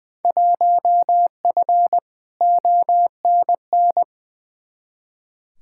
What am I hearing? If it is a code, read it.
Morse code, heard as a keyed tone, about 20 wpm, 700 Hz. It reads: 1F ONN
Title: Field Day Exchanges Encore - 20wpm